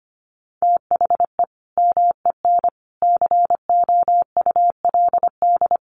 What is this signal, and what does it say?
Morse code, heard as a keyed tone, about 25 wpm, 700 Hz. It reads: THE MEN COULD
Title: Sentences from Top 1200 Words 25wpm